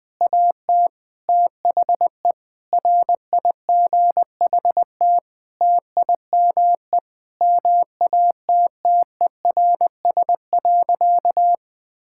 Morse code, heard as a keyed tone, about 21 wpm, 700 Hz. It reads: AT THE RIGHT TIME MATTERS.
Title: Sentences with 9-Word Phrases - Spoken Beginning - 20wpm